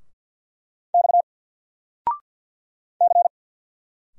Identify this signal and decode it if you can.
Morse code, heard as a keyed tone, about 50 wpm, 700 Hz. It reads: X C
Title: Single Letter-Number 50wpm